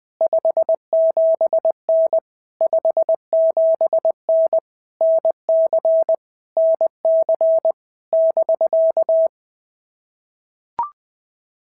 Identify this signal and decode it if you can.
Morse code, heard as a keyed tone, about 20 wpm, 650 Hz. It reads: 57N 57N NC NC <BK>
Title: POTA QSO Practice - 20wpm